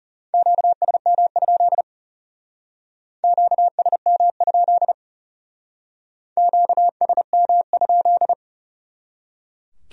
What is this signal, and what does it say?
Morse code, heard as a keyed tone, about 36 wpm, 700 Hz. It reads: QSM? QSM? QSM?
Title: Speed-Racing Q-Codes - 30wpm